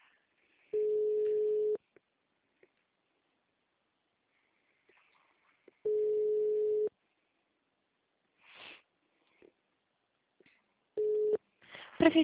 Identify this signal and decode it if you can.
Ringback tone (Europe, 425 Hz), with speech.